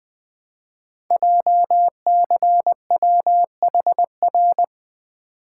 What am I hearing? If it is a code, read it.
Morse code, heard as a keyed tone, about 20 wpm, 700 Hz. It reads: JCWHR
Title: Code Groups - 5 Letters - 20wpm